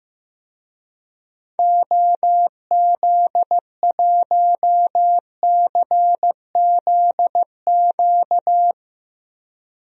Morse code, heard as a keyed tone, about 16 wpm, 700 Hz. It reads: OZ1CZQ
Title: DXCC-100 Call Signs - 6 Characters - Repeated 2x - 15wpm